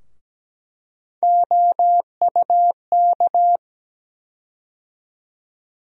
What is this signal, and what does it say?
Morse code, heard as a keyed tone, about 17 wpm, 700 Hz. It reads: OUK
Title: USA Call Sign Postfixes - 3 Letters 17wpm